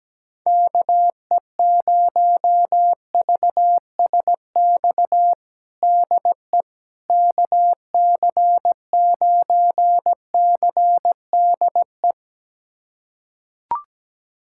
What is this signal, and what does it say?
Morse code, heard as a keyed tone, about 17 wpm, 700 Hz. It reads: KE0VSX DE KC9CDE
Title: USA Callsign DE Callsign 17wpm